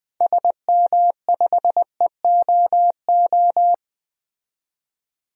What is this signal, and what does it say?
Morse code, heard as a keyed tone, about 21 wpm, 700 Hz. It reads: SM5EOO
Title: DXCC-100 Call Signs - 6 Characters - 20wpm